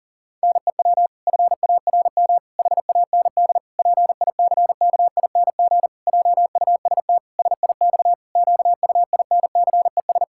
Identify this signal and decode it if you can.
Morse code, heard as a keyed tone, about 37 wpm, 700 Hz. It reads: NEW FARM HAND PICKING JUST SIX QUINCES
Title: Pangram Challenge v2 - 40wpm